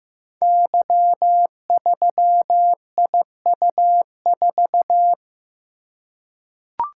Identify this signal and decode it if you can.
Morse code, heard as a keyed tone, about 15 wpm, 700 Hz. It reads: Y3IU4
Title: Code Groups - 5 Letter:Numbers - 15wpm